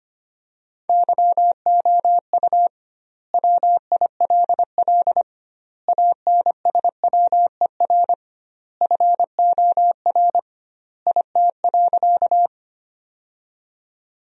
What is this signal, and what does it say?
Morse code, heard as a keyed tone, about 25 wpm, 700 Hz. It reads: YOU WILL ANSWER FOR IT.
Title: 2x Word-Spacing - Sentences from Top 300 Words - 25wpm